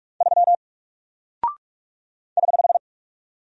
Morse code, heard as a keyed tone, about 44 wpm, 700 Hz. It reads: 3 <HH>
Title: Single Character (Letter:Number:Punctuation:Prosign) Alt 45wpm